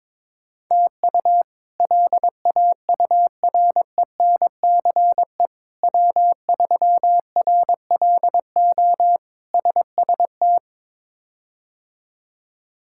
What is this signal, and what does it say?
Morse code, heard as a keyed tone, about 23 wpm, 700 Hz. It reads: TU LAURENCE W3RLO SST
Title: K1USN SST QSO Practice - 22wpm